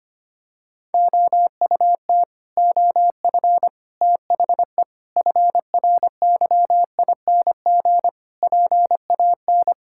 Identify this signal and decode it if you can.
Morse code, heard as a keyed tone, about 25 wpm, 700 Hz. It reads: OUT OF THE FRYING PAN
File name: Sets of 9 Words - 25wpm